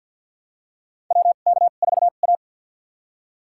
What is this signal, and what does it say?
Morse code, heard as a keyed tone, about 49 wpm, spaced as for 36 wpm, 700 Hz. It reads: WK4A
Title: USA Call Signs - 4 Characters - 2x Character Spacing - 50wpm